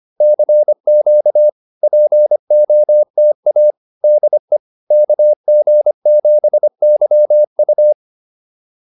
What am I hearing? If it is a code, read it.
Morse code, heard as a keyed tone, about 25 wpm, 600 Hz. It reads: CQ POTA DE KG7YU